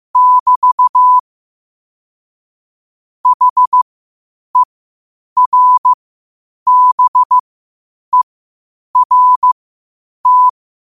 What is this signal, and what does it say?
Morse code, heard as a keyed tone, about 15 wpm, spaced as for 8 wpm, 1000 Hz. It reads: <BT> HERBERT
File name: qotdfiles / 8wpm